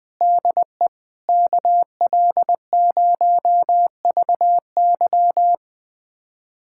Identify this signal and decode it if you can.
Morse code, heard as a keyed tone, about 20 wpm, 700 Hz. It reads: DE KL0VY
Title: USA Callsign DE Callsign 20wpm